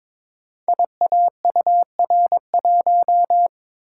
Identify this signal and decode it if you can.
Morse code, heard as a keyed tone, about 22 wpm, 690 Hz. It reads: IAUR1
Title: Code Groups - 5 Letter:Numbers - 22wpm